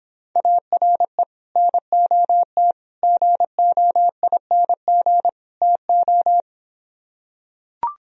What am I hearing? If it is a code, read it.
Morse code, heard as a keyed tone, about 26 wpm, 700 Hz. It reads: ARE NOT GOING TO